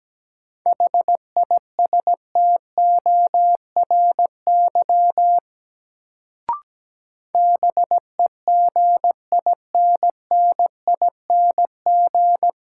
Morse code, heard as a keyed tone, about 17 wpm, 700 Hz. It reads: HISTORY BEGINNING
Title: Sets of 3 Words - New Word Drill - 17wpm